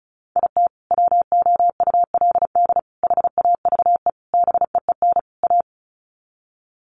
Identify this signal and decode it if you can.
Morse code, heard as a keyed tone, about 34 wpm, 700 Hz. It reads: IT WOULD HAVE BEEN A
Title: Sets of 5 Words Encore 35wpm